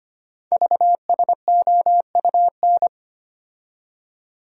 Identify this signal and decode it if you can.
Morse code, heard as a keyed tone, about 25 wpm, 700 Hz. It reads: VSOUN
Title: Code Groups - 5 Letters - 25wpm